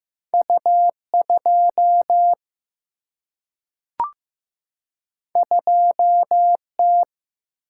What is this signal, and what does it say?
Morse code, heard as a keyed tone, about 15 wpm, 700 Hz. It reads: U2 2T
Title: Single-Letter Single-Number 15wpm